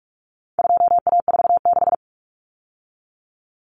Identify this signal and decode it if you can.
Morse code, heard as a keyed tone, about 44 wpm, 700 Hz. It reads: 2A46
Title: Common Cut Numbers - 4-Digit 45wpm